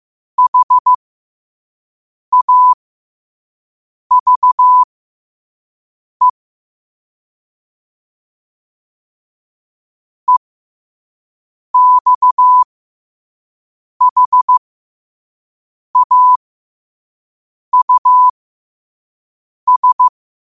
Morse code, heard as a keyed tone, about 15 wpm, spaced as for 5 wpm, 1000 Hz. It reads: HAVE EXHAUS
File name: 5wpm